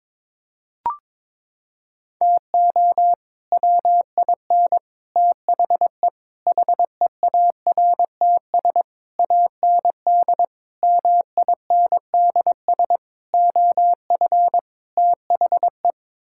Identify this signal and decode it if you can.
Morse code, heard as a keyed tone, about 22 wpm, 700 Hz. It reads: TO WIN THE HEARTS AND MINDS OF THE
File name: practice / 22wpm / Sets of 8 Words - 22wpm